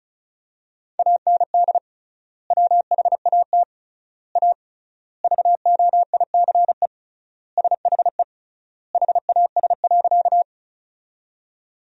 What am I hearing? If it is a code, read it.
Morse code, heard as a keyed tone, about 36 wpm, 700 Hz. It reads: AND WHAT A VOICE SHE HAS.
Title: 3x Word-Spacing - Sentences from Top 500 Words - 35wpm